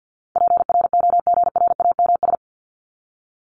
Morse code, heard as a keyed tone, about 50 wpm, 700 Hz. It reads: PROGRAMS
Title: ES:S Suffix - Top 100 Words with Base Words 50wpm